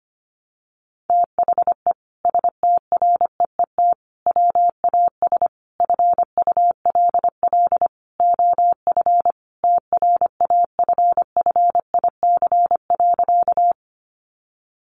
Morse code, heard as a keyed tone, about 25 wpm, 700 Hz. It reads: THE STREET WAS FULL OF TRAFFIC.
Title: Sentences from Top 1300 Words 25wpm